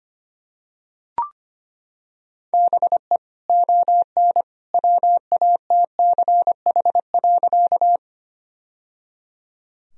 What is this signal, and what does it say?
Morse code, heard as a keyed tone, about 25 wpm, 700 Hz. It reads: BE ON WATCH.
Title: Sentences from Top 1000 Words - Review Part 1 - 25wpm